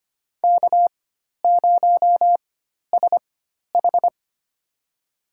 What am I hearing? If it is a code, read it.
Morse code, heard as a keyed tone, about 25 wpm, spaced as for 12 wpm, 700 Hz. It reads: K0SH
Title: USA Call Signs - 4 Characters - 4x Character Spacing - 25wpm